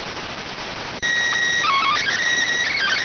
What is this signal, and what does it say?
SSTV (slow-scan television), PD 290.